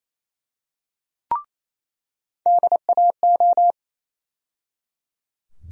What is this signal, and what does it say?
Morse code, heard as a keyed tone, about 28 wpm, 700 Hz. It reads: DAO
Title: USA Call Sign Postfixes - 3 Letters 28wpm